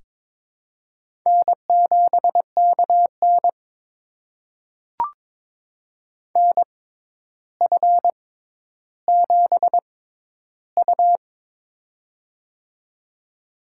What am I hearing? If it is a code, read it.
Morse code, heard as a keyed tone, about 22 wpm, spaced as for 11 wpm, 700 Hz. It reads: N7KN NF7U
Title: USA Call Signs - 4 Characters - 6x Character Spacing - 22wpm